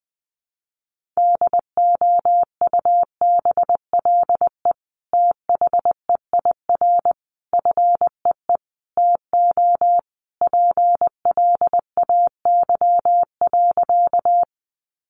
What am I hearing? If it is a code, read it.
Morse code, heard as a keyed tone, about 20 wpm, 700 Hz. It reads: DOUBLE THEIR FEE TO PLAY.
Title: Sentences from Top 1500 Words 20wpm